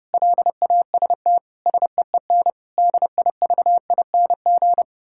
Morse code, heard as a keyed tone, about 31 wpm, 700 Hz. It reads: LAST SEEN DIVING
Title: Pangram Challenge v2 - 30wpm